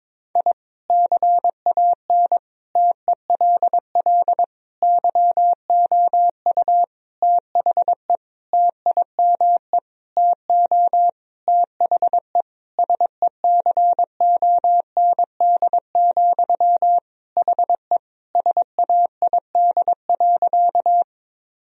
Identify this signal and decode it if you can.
Morse code, heard as a keyed tone, about 22 wpm, 700 Hz. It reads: I CAN TELL YOU THE TIME TO THE SECOND, HE SAID.
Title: Sentences from Top 1000 Words - Review Part 1 - 22wpm